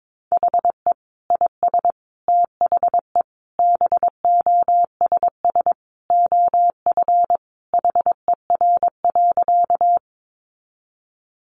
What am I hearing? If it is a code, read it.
Morse code, heard as a keyed tone, about 22 wpm, 700 Hz. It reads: HE IS THE BOSS OF HER.
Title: Sentences from Top 700 Words 22wpm